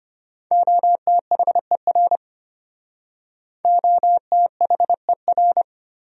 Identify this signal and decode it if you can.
Morse code, heard as a keyed tone, about 32 wpm, 700 Hz. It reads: OTHER OTHER